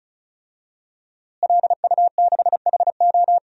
Morse code, heard as a keyed tone, about 34 wpm, 700 Hz. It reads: LU6HO
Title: DXCC-100 Call Signs - 5 Characters - Repeated 3x - 35wpm